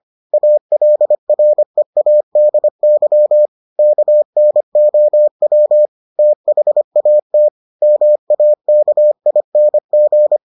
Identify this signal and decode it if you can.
Morse code, heard as a keyed tone, about 25 wpm, 600 Hz. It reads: ALREADY KNOW THAT MAKING